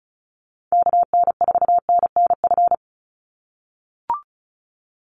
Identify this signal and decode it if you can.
Morse code, heard as a keyed tone, about 34 wpm, 700 Hz. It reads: KN4NNF
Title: USA Call Signs 35wpm